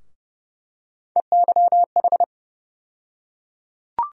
Morse code, heard as a keyed tone, about 29 wpm, 700 Hz. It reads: EYH
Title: USA Call Sign Postfixes - 3 Letters 30wpm